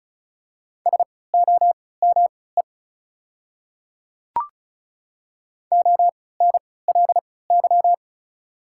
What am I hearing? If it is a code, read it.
Morse code, heard as a keyed tone, about 34 wpm, spaced as for 20 wpm, 690 Hz. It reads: SOME ONLY
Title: 3x Character Spacing - Top 500 Words - 35wpm